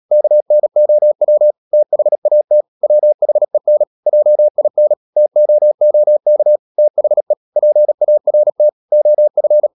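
Morse code, heard as a keyed tone, about 37 wpm, 600 Hz. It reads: KNOW THAT WHEN JIN TOOK THE PART OF
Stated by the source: Morse code practice at 35–39 wpm sending English text of Ken Follett's novel Never, from CW med Gnister.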